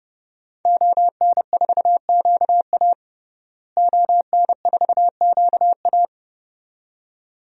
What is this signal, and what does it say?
Morse code, heard as a keyed tone, about 30 wpm, 700 Hz. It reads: ON4QA ON4QA
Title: DXCC-100 Call Signs - 5 Characters - Repeated 2x - 30wpm